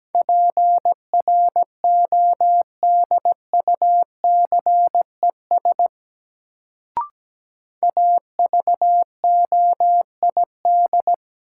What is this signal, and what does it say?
Morse code, heard as a keyed tone, about 17 wpm, 700 Hz. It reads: PRODUCES AVOID
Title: Sentences from Top 1000 Words - Review Part 6 - New Word Drill - 17wpm